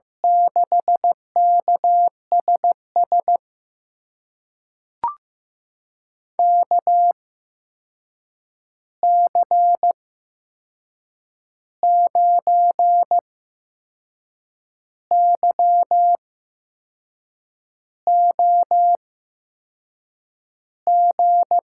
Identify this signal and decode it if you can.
Morse code, heard as a keyed tone, about 15 wpm, spaced as for 6 wpm, 700 Hz. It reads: 6KSS KC9YOG
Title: USA Call Signs - 6 Characters - 8x Character Spacing - 15wpm